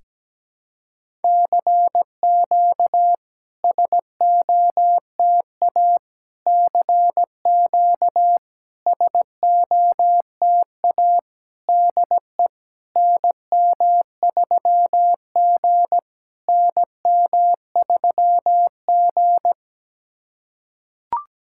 Morse code, heard as a keyed tone, about 17 wpm, 700 Hz. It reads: CQ SOTA CQ SOTA DE NM3G NM3G
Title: SOTA QSO Practice 17wpm